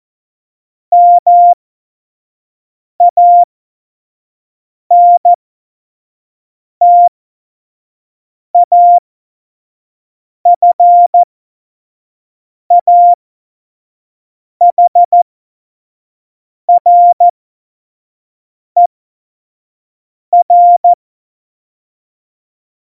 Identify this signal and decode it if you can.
Morse code, heard as a keyed tone, about 14 wpm, spaced as for 5 wpm, 700 Hz. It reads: MANTAFAHRER
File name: Morsekurs / Witze / Witz 07 Juli 2020, 25 F70